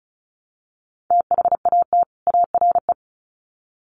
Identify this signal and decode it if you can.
Morse code, heard as a keyed tone, about 34 wpm, 700 Hz. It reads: THAT ARE
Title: Sets of 2 Words 35wpm